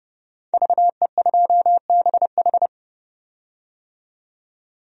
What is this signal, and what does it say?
Morse code, heard as a keyed tone, about 31 wpm, 700 Hz. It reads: VE2BH
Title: DXCC-100 Call Signs - 5 Characters - 30wpm